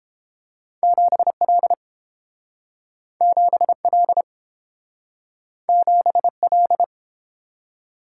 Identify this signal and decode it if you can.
Morse code, heard as a keyed tone, about 30 wpm, 700 Hz. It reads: 7L 7L 7L